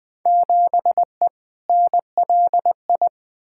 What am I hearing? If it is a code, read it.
Morse code, heard as a keyed tone, about 20 wpm, 700 Hz. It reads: 7E NLI
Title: Field Day Exchanges Encore - 20wpm